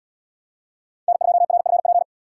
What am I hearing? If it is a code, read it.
Morse code, heard as a keyed tone, about 77 wpm, 700 Hz. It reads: N2RFP